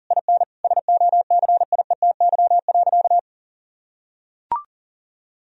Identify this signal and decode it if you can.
Morse code, heard as a keyed tone, about 40 wpm, 700 Hz. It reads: IN SOCIETY.
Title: Sentences from Top 1000 Words - Review Part 4 - 40wpm